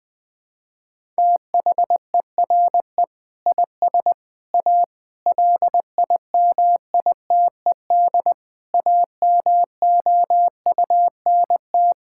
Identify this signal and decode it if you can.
Morse code, heard as a keyed tone, about 20 wpm, 700 Hz. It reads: THERE IS A LIMITED AMOUNT
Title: Sentences from Top 1000 Words - Review Part 6 - 20wpm